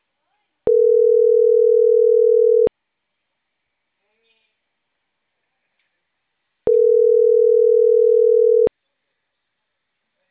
Ringback tone (North America), 440 + 480 Hz.